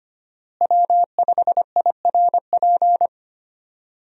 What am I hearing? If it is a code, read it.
Morse code, heard as a keyed tone, about 25 wpm, 700 Hz. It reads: W5IRP
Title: USA Call Signs - 5 Characters - Repeated 3x - 25wpm